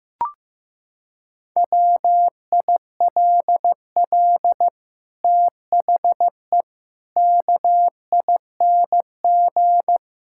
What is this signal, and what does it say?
Morse code, heard as a keyed tone, about 15 wpm, 700 Hz. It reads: WILL THE KING
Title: Sentences from Top 1000 Words - Review Part 6 - 15wpm